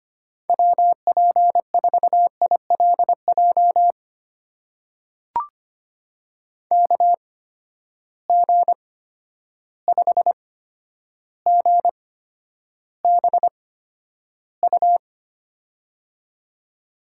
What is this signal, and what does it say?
Morse code, heard as a keyed tone, about 25 wpm, spaced as for 11 wpm, 700 Hz. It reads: WP4ILJ KG5GBU
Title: USA Call Signs - 6 Characters - 8x Character Spacing - 25wpm